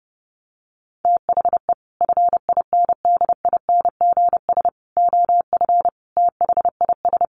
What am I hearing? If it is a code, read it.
Morse code, heard as a keyed tone, about 30 wpm, 700 Hz. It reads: THE FINDINGS OF THIS
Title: Sets of 5 Words Encore2 30wpm